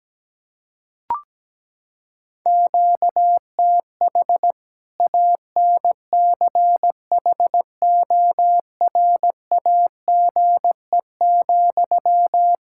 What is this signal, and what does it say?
Morse code, heard as a keyed tone, about 17 wpm, 700 Hz. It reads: QTH ANCHORAGE,
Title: K1USN SST QSO Practice 17wpm